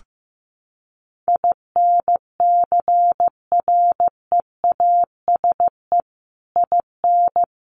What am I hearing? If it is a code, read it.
Morse code, heard as a keyed tone, about 15 wpm, 700 Hz. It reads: INCREASE IN
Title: Sets of 5 Words 15wpm